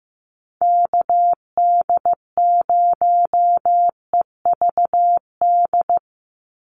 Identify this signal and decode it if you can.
Morse code, heard as a keyed tone, about 15 wpm, 700 Hz. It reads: KD0EVD
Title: USA Call Signs 15wpm